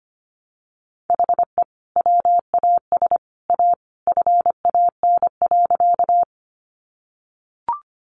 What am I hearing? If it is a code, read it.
Morse code, heard as a keyed tone, about 25 wpm, 700 Hz. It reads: HE WAS A FAN.
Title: Sentences from Top 1300 Words 25wpm